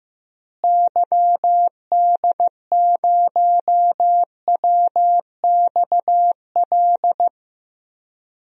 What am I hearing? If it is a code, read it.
Morse code, heard as a keyed tone, about 15 wpm, 700 Hz. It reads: YD0WXL
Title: DXCC-100 Call Signs - 6 Characters - Repeated 2x - 15wpm